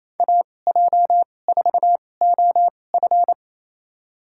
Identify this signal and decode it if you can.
Morse code, heard as a keyed tone, about 28 wpm, spaced as for 20 wpm, 700 Hz. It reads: AJ4OF
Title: USA Call Signs - 5 Characters - 2x Character Spacing - 28wpm